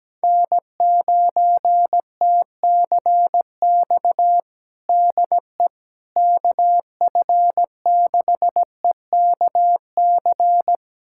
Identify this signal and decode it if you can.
Morse code, heard as a keyed tone, about 17 wpm, 700 Hz. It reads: N9TCX DE KF6EKC
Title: USA Callsign DE Callsign 17wpm